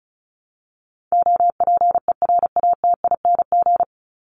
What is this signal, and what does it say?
Morse code, heard as a keyed tone, about 34 wpm, 700 Hz. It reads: OPERATING
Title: ING Suffix - Top 100 Words with Base Words 35wpm